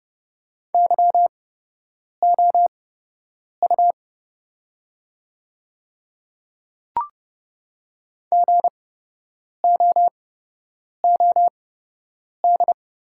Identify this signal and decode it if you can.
Morse code, heard as a keyed tone, about 30 wpm, spaced as for 8 wpm, 700 Hz. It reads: YOU GOOD
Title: 8x Character Spacing - Top 300 Words - 30wpm